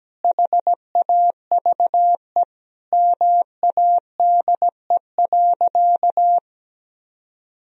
Morse code, heard as a keyed tone, about 17 wpm, 700 Hz. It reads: HAVE MADE.
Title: Sentences from Top 1000 Words - Review Part 2 - 17wpm